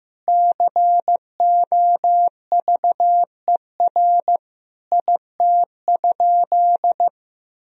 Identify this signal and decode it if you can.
Morse code, heard as a keyed tone, about 15 wpm, 700 Hz. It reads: COVER IT?
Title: Sentences from Top 1000 Words - Review Part 1 - 15wpm